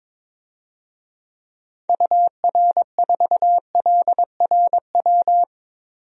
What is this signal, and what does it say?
Morse code, heard as a keyed tone, about 21 wpm, 700 Hz. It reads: UR4LRW
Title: DXCC-100 Call Signs - 6 Characters - Repeated 3x - 22wpm